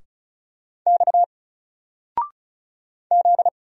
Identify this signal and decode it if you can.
Morse code, heard as a keyed tone, about 36 wpm, 700 Hz. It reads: X Z
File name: Single-Letter-Number-35wpm